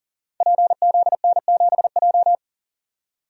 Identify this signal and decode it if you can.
Morse code, heard as a keyed tone, about 37 wpm, 700 Hz. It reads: PZN7J
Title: Code Groups - 5 Letter:Numbers - 40wpm